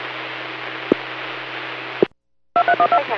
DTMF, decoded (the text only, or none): 2312